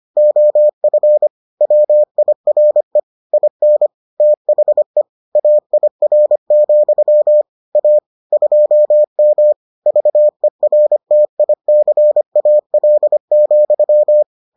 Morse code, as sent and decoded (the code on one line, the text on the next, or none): --- ..-. / .-- .. .-. / .. -. / - .... / .- .. .-. --..-- / .- / ..--- -- / ...- . .-. - .. -.-. .- .-.. --..--
OF WIRE IN THE AIR, A 2M VERTICAL,